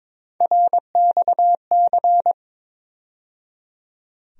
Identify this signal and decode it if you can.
Morse code, heard as a keyed tone, about 22 wpm, 700 Hz. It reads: RXC
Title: USA Call Sign Postfixes - 3 Letters 22wpm